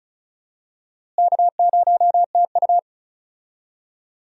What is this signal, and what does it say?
Morse code, heard as a keyed tone, about 34 wpm, 700 Hz. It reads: K0TU
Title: USA Call Signs - 4 Characters - Repeated 2x - 35wpm